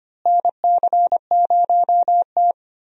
Morse code, heard as a keyed tone, about 25 wpm, 700 Hz. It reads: NC0T
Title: USA Call Signs - 4 Characters - Repeated 3x - 25wpm